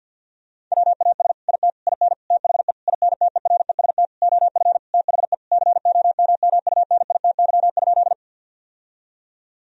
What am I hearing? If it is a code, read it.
Morse code, heard as a keyed tone, about 50 wpm, 700 Hz. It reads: WAS IT IN THE INTEREST OF THE COMMUNITY?
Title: Sentences from Top 1000 Words - Review Part 4 - 50wpm